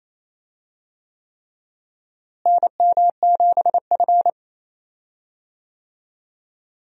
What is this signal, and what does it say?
Morse code, heard as a keyed tone, about 27 wpm, 700 Hz. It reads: NM7F
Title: USA Call Signs - 4 Characters - Repeated 3x - 28wpm